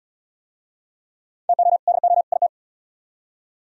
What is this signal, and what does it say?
Morse code, heard as a keyed tone, about 68 wpm, 700 Hz. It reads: TO DO IT